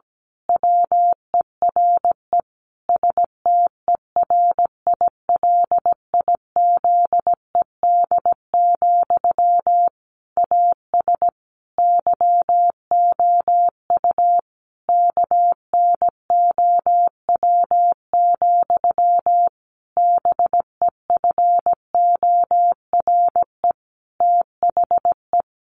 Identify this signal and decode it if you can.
Morse code, heard as a keyed tone, about 17 wpm, 700 Hz. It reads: WERE STERILIZED, AS YOU KNOW, BEFORE THE
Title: A Sound Of Thunder 17wpm